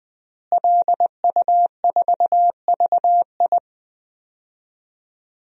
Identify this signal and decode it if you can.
Morse code, heard as a keyed tone, about 20 wpm, 700 Hz. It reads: LU4VI
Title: DXCC-100 Call Signs - 5 Characters - Repeated 2x - 20wpm